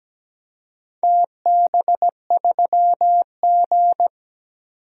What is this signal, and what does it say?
Morse code, heard as a keyed tone, about 17 wpm, 700 Hz. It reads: TB3G
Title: DXCC-100 Call Signs - 4 Characters - 17wpm